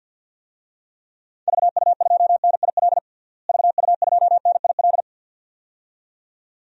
Morse code, heard as a keyed tone, about 48 wpm, 700 Hz. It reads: VU2NIL VU2NIL
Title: DXCC-100 Call Signs - 6 Characters - Repeated 2x - 50wpm